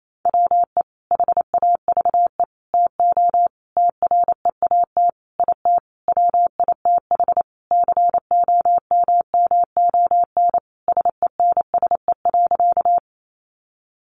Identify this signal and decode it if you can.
Morse code, heard as a keyed tone, about 28 wpm, 700 Hz. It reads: WE HAVE TO TREAT IT WITH COMMON SENSE.
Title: Sentences from Top 1300 Words 28wpm